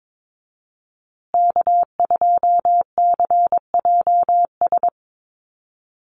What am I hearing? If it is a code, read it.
Morse code, heard as a keyed tone, about 22 wpm, 700 Hz. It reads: K2CJS
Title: USA Call Signs Repeated 3x 22wpm